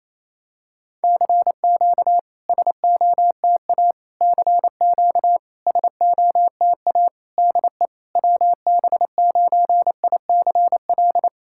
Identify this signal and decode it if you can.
Morse code, heard as a keyed tone, about 28 wpm, 700 Hz. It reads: CQ SOTA CQ SOTA DE WB9ICL
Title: SOTA QSO Practice 28wpm